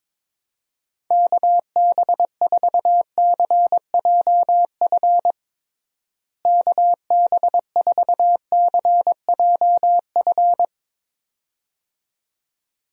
Morse code, heard as a keyed tone, about 22 wpm, 700 Hz. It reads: KB4CJF KB4CJF
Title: USA Call Signs - 6 Characters - Repeated 2x - 22wpm